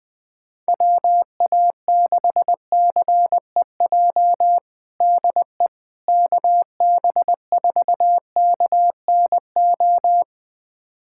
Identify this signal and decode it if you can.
Morse code, heard as a keyed tone, about 20 wpm, 700 Hz. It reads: WA6CEJ DE KB4KNO
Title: USA Callsign DE Callsign - 6 Characters - 20wpm